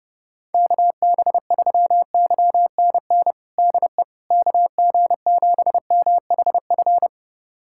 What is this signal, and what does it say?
Morse code, heard as a keyed tone, about 31 wpm, 700 Hz. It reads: KB3YNN DE KG7MHF
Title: USA Callsign DE Callsign - 6 Characters - 30wpm